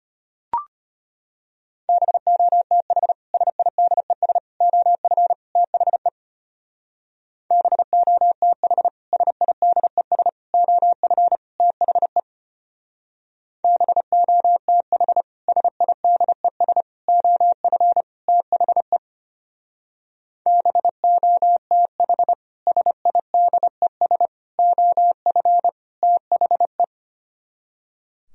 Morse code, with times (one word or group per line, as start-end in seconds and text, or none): 1.9-3.1 BOTH
3.3-4.4 SIDES
4.6-5.3 OF
5.6-6.1 THE
7.5-8.9 BOTH
9.1-10.3 SIDES
10.5-11.4 OF
11.6-12.2 THE
13.6-15.2 BOTH
15.5-16.8 SIDES
17.1-18.0 OF
18.3-19.0 THE
20.5-22.3 BOTH
22.7-24.3 SIDES
24.6-25.7 OF
26.0-26.8 THE